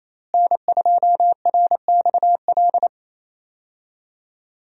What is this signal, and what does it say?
Morse code, heard as a keyed tone, about 28 wpm, 700 Hz. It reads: N2RXL
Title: USA Call Signs - 5 Characters - Repeated 2x - 28wpm